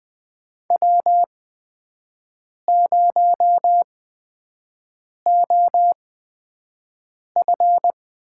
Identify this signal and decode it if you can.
Morse code, heard as a keyed tone, about 20 wpm, spaced as for 5 wpm, 700 Hz. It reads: W0OF
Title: USA Call Signs - 8x Character Spacing - 20wpm